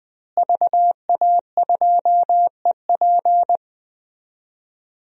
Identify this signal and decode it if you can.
Morse code, heard as a keyed tone, about 20 wpm, 700 Hz. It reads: VA2EP
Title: DXCC-100 Call Signs - 5 Characters - Repeated 2x - 20wpm